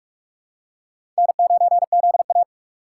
Morse code, heard as a keyed tone, about 46 wpm, 700 Hz. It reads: N9ZA
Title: USA Call Signs - 3x Character Spacing - 45wpm